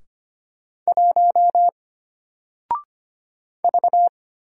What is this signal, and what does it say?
Morse code, heard as a keyed tone, about 25 wpm, 700 Hz. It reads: 1 V
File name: Single-Letter-Number-25wpm